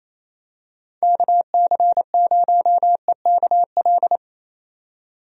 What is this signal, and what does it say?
Morse code, heard as a keyed tone, about 28 wpm, 700 Hz. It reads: KC0EKL
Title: USA Call Signs - 6 Characters - 28wpm